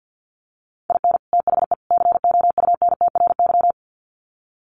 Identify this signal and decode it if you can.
Morse code, heard as a keyed tone, about 50 wpm, 700 Hz. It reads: IN THE COUNTRY
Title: Sets of 3 Words 50wpm